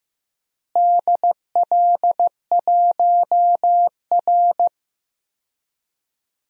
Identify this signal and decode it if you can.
Morse code, heard as a keyed tone, about 15 wpm, 700 Hz. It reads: DL1R
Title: DXCC-100 Call Signs - 4 Characters - Repeated 3x - 15wpm